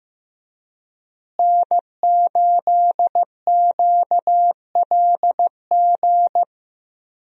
Morse code, heard as a keyed tone, about 15 wpm, 700 Hz. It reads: N8QLG
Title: USA Call Signs - 5 Characters - Repeated 2x - 15wpm